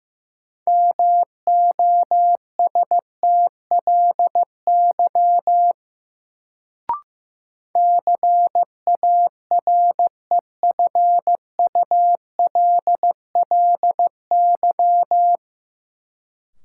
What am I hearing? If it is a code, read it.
Morse code, heard as a keyed tone, about 15 wpm, 700 Hz. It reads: MOSTLY CAREFULLY
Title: Sentences from Top 1300 Words - New Word Drill - 15wpm